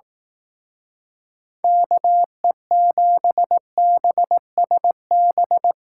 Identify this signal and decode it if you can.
Morse code, heard as a keyed tone, about 18 wpm, 700 Hz. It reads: KE7BSB
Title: Speed Racing - USA 2x3 Call Signs 15wpm